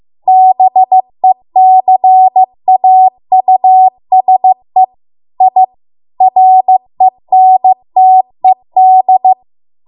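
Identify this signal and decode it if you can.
Morse code, heard as a keyed tone, about 15 wpm, 750 Hz. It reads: BECAUSE I RENTED
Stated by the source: Morse code practice at 15 wpm from the national amateur radio association's code archive.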